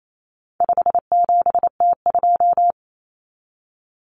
Morse code, heard as a keyed tone, about 28 wpm, 700 Hz. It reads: 57T2
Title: Common Cut Numbers - 4-Digit 28wpm